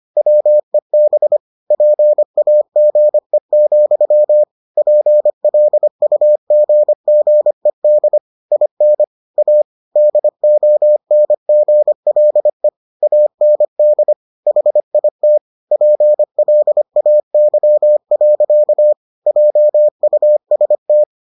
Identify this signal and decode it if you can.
Morse code, heard as a keyed tone, about 25 wpm, 600 Hz. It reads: WEB PAGE, PLUGGED IN A DONGLE AND HIT PLAY. JUST